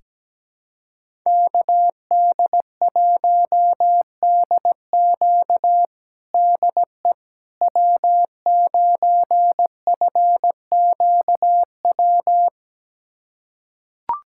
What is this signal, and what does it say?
Morse code, heard as a keyed tone, about 17 wpm, 700 Hz. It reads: KD1DQ DE W9FQW
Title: USA Callsign DE Callsign - 5 Characters - 17wpm